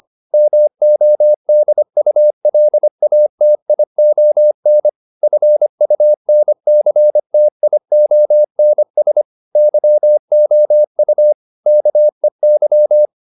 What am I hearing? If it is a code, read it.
Morse code, heard as a keyed tone, about 25 wpm, 600 Hz. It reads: MODULATION FUNCTIONS YOU KEY